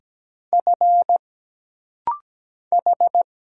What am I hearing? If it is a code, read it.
Morse code, heard as a keyed tone, about 17 wpm, 700 Hz. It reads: F H
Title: Single Letter-Number - Rapid-Fire 17wpm